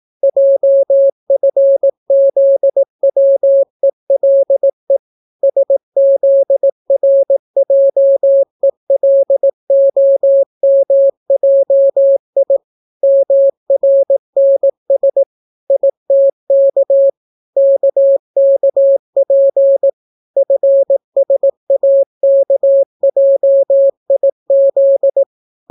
Morse code, heard as a keyed tone, about 18 wpm, 550 Hz. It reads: JFZWELE SZRJELOMJI MRNS ITK KKP FSAKJIZ